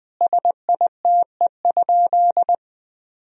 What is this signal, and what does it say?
Morse code, heard as a keyed tone, about 20 wpm, 700 Hz. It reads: SITE?
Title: Sentences from Top 1000 Words - Review Part 4 - 20wpm